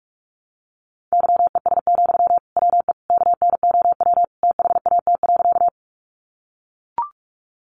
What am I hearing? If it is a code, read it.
Morse code, heard as a keyed tone, about 44 wpm, 700 Hz. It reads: YES, WE KNOW THAT.
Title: Sentences from Top 500 Words 45wpm